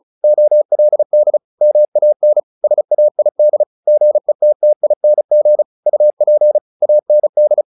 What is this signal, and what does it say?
Morse code, heard as a keyed tone, about 34 wpm, 600 Hz. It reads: OLD MAN SAID GETTING UP AND
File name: the old man and the sea / the old man and the sea 1 35wpm